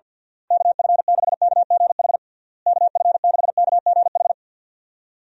Morse code, heard as a keyed tone, about 53 wpm, 700 Hz. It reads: KF6KGH KF6KGH
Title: USA Call Signs - 6 Characters - Repeated 3x - 50wpm